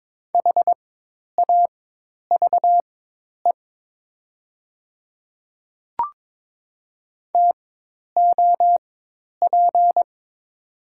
Morse code, heard as a keyed tone, about 22 wpm, spaced as for 10 wpm, 700 Hz. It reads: HAVE TOP
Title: 4x Character Spacing - Top 500 Words - 22wpm